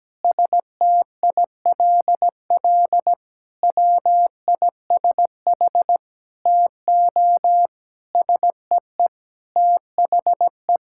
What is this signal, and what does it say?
Morse code, heard as a keyed tone, about 17 wpm, 700 Hz. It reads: STILL WISH TO SEE THE
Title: Sentences from Top 1600 Words 17wpm